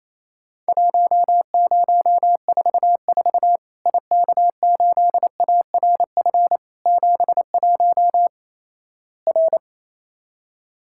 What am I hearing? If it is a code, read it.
Morse code, heard as a keyed tone, about 28 wpm, 700 Hz. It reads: 1044 IK8ARF 71 R
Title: Worked All Europe DX Contest - QTC Exchange - 28wpm